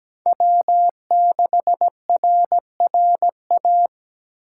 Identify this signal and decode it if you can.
Morse code, heard as a keyed tone, about 20 wpm, 700 Hz. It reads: W6RRA
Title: Speed Racing - USA 1x3 Call Signs 17wpm